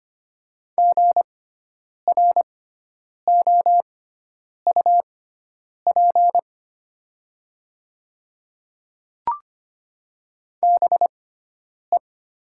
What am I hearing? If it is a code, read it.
Morse code, heard as a keyed tone, about 25 wpm, spaced as for 9 wpm, 700 Hz. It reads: GROUP BE
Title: 6x Character Spacing - Top 300 Words - 25wpm